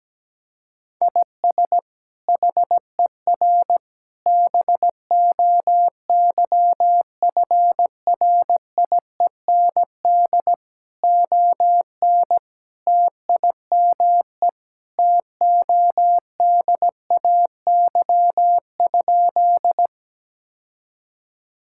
Morse code, heard as a keyed tone, about 17 wpm, 700 Hz. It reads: IS HER BOYFRIEND ON TIME TODAY?
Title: Sentences from Top 1000 Words - Review Part 5 - 17wpm